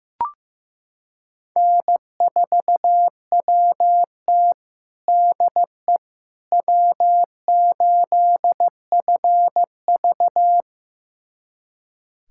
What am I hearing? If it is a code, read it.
Morse code, heard as a keyed tone, about 15 wpm, 700 Hz. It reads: N4WT DE W8FV
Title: USA Callsign DE Callsign - 4 Characters - 15wpm